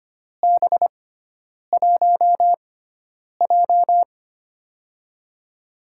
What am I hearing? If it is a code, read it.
Morse code, heard as a keyed tone, about 25 wpm, spaced as for 9 wpm, 700 Hz. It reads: B1J
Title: USA Call Signs - 4 Characters - 6x Character Spacing - 25wpm